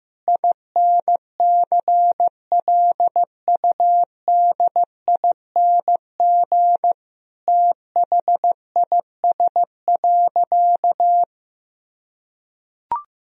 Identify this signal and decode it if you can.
Morse code, heard as a keyed tone, about 16 wpm, 700 Hz. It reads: INCLUDING THIS.
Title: Sentences from Top 1000 Words - Review Part 4 - 15wpm